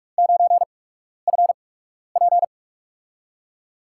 Morse code, heard as a keyed tone, about 44 wpm, spaced as for 12 wpm, 700 Hz. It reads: 9FP
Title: USA Call Signs - 8x Character Spacing - 45wpm